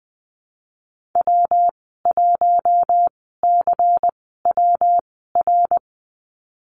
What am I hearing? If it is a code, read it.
Morse code, heard as a keyed tone, about 20 wpm, spaced as for 14 wpm, 700 Hz. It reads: W1CWR
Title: 147 Lesson 24g - Letter y - Call Signs 20wpm